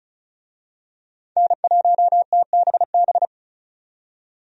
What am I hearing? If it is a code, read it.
Morse code, heard as a keyed tone, about 36 wpm, 700 Hz. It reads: N1TBB
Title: USA Call Signs - 5 Characters - 35wpm